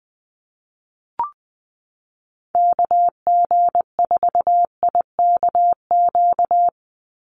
Morse code, heard as a keyed tone, about 20 wpm, 700 Hz. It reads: KG4IKQ
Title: Call Signs Repeated 2x - 20wpm